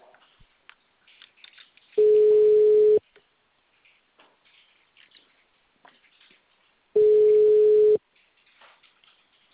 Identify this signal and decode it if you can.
Ringback tone (Europe, 425 Hz).
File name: out- -2015- - - .6